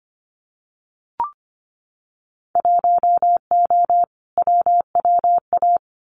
Morse code, heard as a keyed tone, about 25 wpm, 690 Hz. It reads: 1O WWA
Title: Winter Field Day Exchanges 25wpm